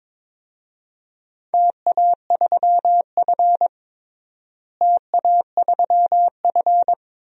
Morse code, heard as a keyed tone, about 21 wpm, 700 Hz. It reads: TA3F TA3F
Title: DXCC-100 Call Signs - 4 Characters - Repeated 2x - 22wpm